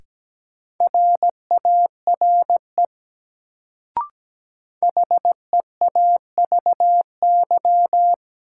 Morse code, heard as a keyed tone, about 17 wpm, 700 Hz. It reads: RARE HEAVY
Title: Sentences from Top 1100 Words - New Word Drill Alt - 17wpm